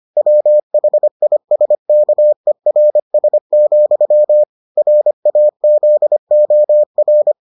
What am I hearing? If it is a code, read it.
Morse code, heard as a keyed tone, about 25 wpm, 600 Hz. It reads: WHISKERS, RAZOR